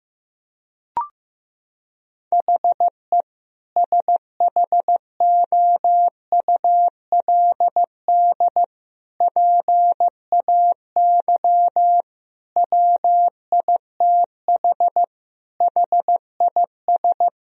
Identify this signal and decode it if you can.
Morse code, heard as a keyed tone, about 15 wpm, 700 Hz. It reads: HE SHOULD PAY WITH HIS
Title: Sentences from Top 1000 Words - Review Part 2 - 15wpm